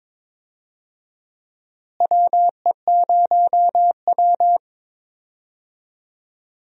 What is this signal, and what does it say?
Morse code, heard as a keyed tone, about 22 wpm, 700 Hz. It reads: WE0W
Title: USA Call Signs - 4 Characters - Repeated 3x - 22wpm